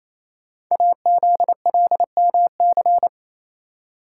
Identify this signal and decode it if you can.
Morse code, heard as a keyed tone, about 27 wpm, 700 Hz. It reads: AZLMC
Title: Code Groups - 5 Letters - 28wpm